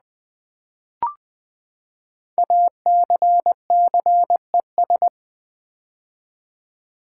Morse code, heard as a keyed tone, about 20 wpm, 700 Hz. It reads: ACCES
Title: Build A Word - 5 Letters At A Time - 10-Letter Words - 20wpm